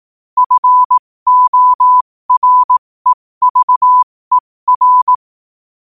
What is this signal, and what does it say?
Morse code, heard as a keyed tone, about 18 wpm, spaced as for 15 wpm, 1000 Hz. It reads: FOREVER